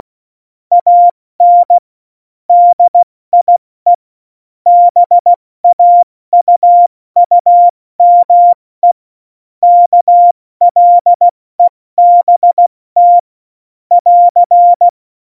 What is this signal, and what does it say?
Morse code, heard as a keyed tone, about 16 wpm, 700 Hz. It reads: AN DIE BAUUME KLEBT <AR>
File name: Morsekurs / Witze / Witz 29. Mai 2018, 70 F80